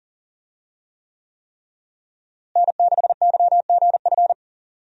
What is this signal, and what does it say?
Morse code, heard as a keyed tone, about 40 wpm, 700 Hz. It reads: N6YGF
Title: USA Call Signs - 5 Characters - Repeated 3x - 40wpm